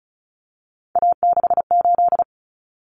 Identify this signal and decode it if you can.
Morse code, heard as a keyed tone, about 34 wpm, 700 Hz. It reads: A68
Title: Common Cut Numbers - 3-Digit 35wpm